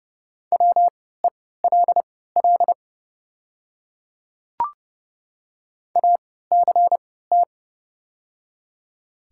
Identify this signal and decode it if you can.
Morse code, heard as a keyed tone, about 29 wpm, spaced as for 17 wpm, 690 Hz. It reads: WELL ACT
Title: 3x Character Spacing - Top 200 Words - 30wpm